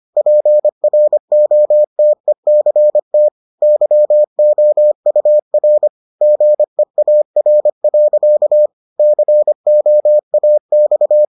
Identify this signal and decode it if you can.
Morse code, heard as a keyed tone, about 25 wpm, 600 Hz. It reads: PROTECT YOUR GEAR. COAX